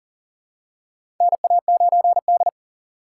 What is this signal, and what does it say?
Morse code, heard as a keyed tone, about 40 wpm, 700 Hz. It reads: NA9D
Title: USA Call Signs - 4 Characters - 3x Character Spacing - 40wpm